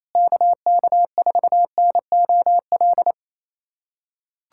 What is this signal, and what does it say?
Morse code, heard as a keyed tone, about 28 wpm, 700 Hz. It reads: KK4NOL
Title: USA Callsign DE Callsign 28wpm